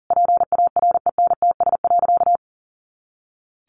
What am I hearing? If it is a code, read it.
Morse code, heard as a keyed tone, about 37 wpm, 700 Hz. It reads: PARENTS.
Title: Easy Phrases 40wpm